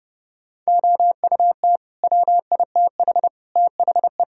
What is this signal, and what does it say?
Morse code, heard as a keyed tone, about 30 wpm, 700 Hz. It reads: OUT WITH THE
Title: Sentences from Top 1000 Words - Review Part 2 - 30wpm